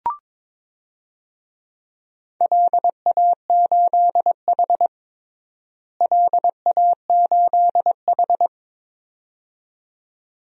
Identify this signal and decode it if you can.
Morse code, heard as a keyed tone, about 21 wpm, 700 Hz. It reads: LA8H LA8H
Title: DXCC-100 Call Signs - 4 Characters - Repeated 2x - 22wpm